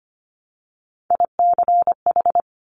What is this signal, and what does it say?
Morse code, heard as a keyed tone, about 25 wpm, 700 Hz. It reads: ICH
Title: Top Three Letter Combinations in Words 25wpm